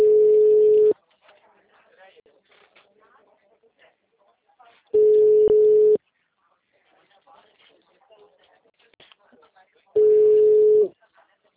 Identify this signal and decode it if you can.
Ringback tone (Europe, 425 Hz), with speech.